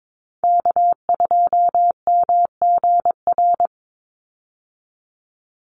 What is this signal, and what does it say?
Morse code, heard as a keyed tone, about 22 wpm, 700 Hz. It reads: K2MGR
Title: Call Signs Repeated 3x - 22wpm